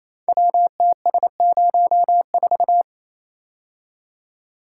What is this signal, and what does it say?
Morse code, heard as a keyed tone, about 28 wpm, 700 Hz. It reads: WTS04
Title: Code Groups - 5 Letter:Numbers - 28wpm